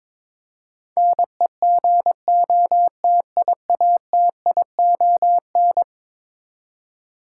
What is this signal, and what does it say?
Morse code, heard as a keyed tone, about 22 wpm, 700 Hz. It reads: NEGOTIATION
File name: practice / 22wpm / Eleven Letter Words - 22wpm